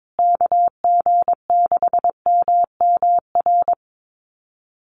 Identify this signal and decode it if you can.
Morse code, heard as a keyed tone, about 22 wpm, 700 Hz. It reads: KG6MMR
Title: USA Call Signs Repeated 3x 22wpm